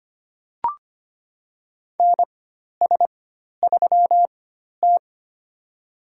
Morse code, heard as a keyed tone, about 25 wpm, spaced as for 12 wpm, 700 Hz. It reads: NS3T
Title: USA Call Signs - 4 Characters - 4x Character Spacing - 25wpm